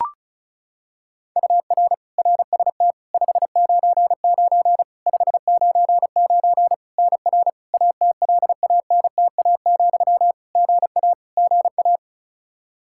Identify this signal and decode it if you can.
Morse code, heard as a keyed tone, about 36 wpm, 700 Hz. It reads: UR RST 599 599 NR ATLANTA, GA GA
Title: Practice QSO - Exchange 1 - Beginning - 35wpm